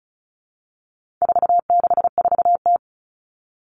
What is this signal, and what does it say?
Morse code, heard as a keyed tone, about 34 wpm, 700 Hz. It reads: 464T
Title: All Cut Numbers - 4-Digit 35wpm